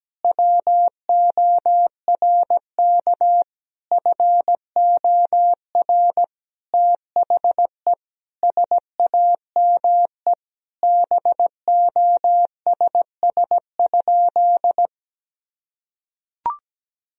Morse code, heard as a keyed tone, about 17 wpm, 700 Hz. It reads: WORK FOR THE SAME BOSS?
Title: Sentences from Top 1000 Words - Review Part 4 - 17wpm